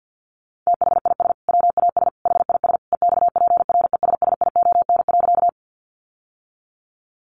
Morse code, heard as a keyed tone, about 48 wpm, 700 Hz. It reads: THIS WAS HIS EXPRESSION.
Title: Sentences from Top 1200 Words 50wpm